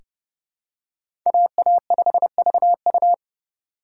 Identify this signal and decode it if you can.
Morse code, heard as a keyed tone, about 30 wpm, 700 Hz. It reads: AA5VU
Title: K1USN SST QSO Practice - 30wpm